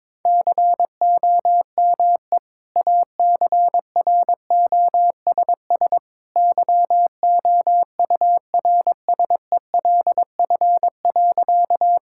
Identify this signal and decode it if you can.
Morse code, heard as a keyed tone, about 21 wpm, 700 Hz. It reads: COME ACROSS YOURSELF.
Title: Sentences from Top 1000 Words - Review Part 5 - 22wpm